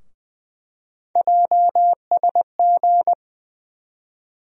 Morse code, heard as a keyed tone, about 20 wpm, 700 Hz. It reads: JSG
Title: USA Call Sign Postfixes - 3 Letters 20wpm